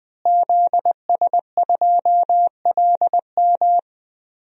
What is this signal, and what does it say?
Morse code, heard as a keyed tone, about 20 wpm, 700 Hz. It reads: ZS2LM
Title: DXCC-100 Call Signs - 5 Characters - Repeated 3x - 20wpm